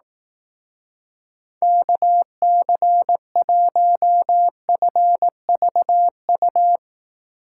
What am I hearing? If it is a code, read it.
Morse code, heard as a keyed tone, about 18 wpm, 700 Hz. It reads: KC1FVU
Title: Speed Racing - USA 2x3 Call Signs 15wpm